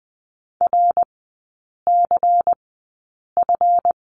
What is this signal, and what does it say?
Morse code, heard as a keyed tone, about 20 wpm, spaced as for 8 wpm, 690 Hz. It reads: RCF
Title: 166 Lesson 27e - Number 7 - Three Characters 20wpm